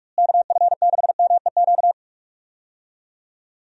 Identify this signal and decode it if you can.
Morse code, heard as a keyed tone, about 46 wpm, 700 Hz. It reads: KF6MEQ
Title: USA Call Signs - 6 Characters - Repeated 2x - 45wpm